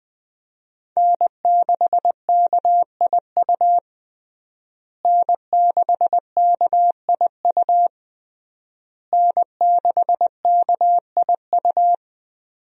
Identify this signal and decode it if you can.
Morse code, heard as a keyed tone, about 20 wpm, 700 Hz. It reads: N6KIU N6KIU N6KIU
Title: USA Call Signs - 5 Characters - Repeated 3x - 20wpm